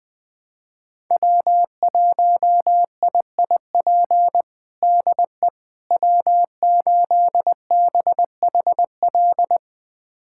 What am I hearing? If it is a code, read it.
Morse code, heard as a keyed tone, about 20 wpm, 700 Hz. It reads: W1IIP DE W8BHL
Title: USA Callsign DE Callsign - 5 Characters - 20wpm